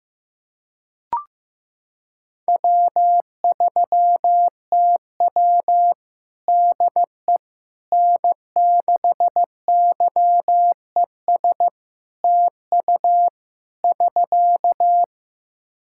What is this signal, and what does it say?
Morse code, heard as a keyed tone, about 15 wpm, 700 Hz. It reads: W3TW DE N6YES TU <SK>
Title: Practice QSO - Exchange 4 - Ending - 15wpm